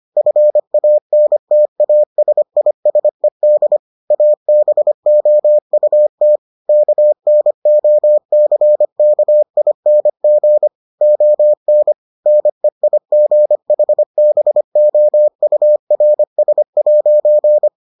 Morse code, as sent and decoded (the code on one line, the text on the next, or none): ..-. .- -. - .- ... .. ... -.. / .- -... --- ..- - / -.- -. --- -.-. -.- .. -. --. / --- -. / -. .. --. .... -... --- ..- .-. ... .----.
FANTASISED ABOUT KNOCKING ON NEIGHBOURS'